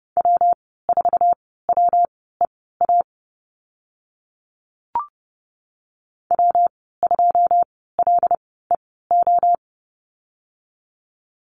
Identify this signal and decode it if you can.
Morse code, heard as a keyed tone, about 30 wpm, spaced as for 17 wpm, 700 Hz. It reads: W4WEA W2LEO
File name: Lesson 19g - Letter w - Call Signs 30wpm